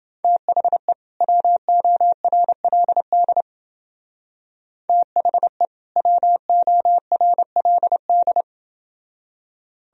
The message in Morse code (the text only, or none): THE WORLD THE WORLD